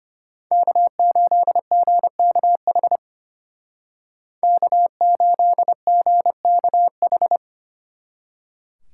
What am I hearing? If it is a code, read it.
Morse code, heard as a keyed tone, about 32 wpm, 700 Hz. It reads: K8GKH K8GKH